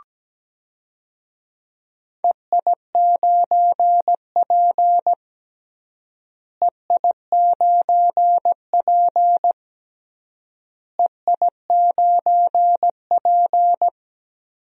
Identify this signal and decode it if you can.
Morse code, heard as a keyed tone, about 17 wpm, 700 Hz. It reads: EI9P EI9P EI9P
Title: DXCC-100 Call Signs - 4 Characters - Repeated 3x - 17wpm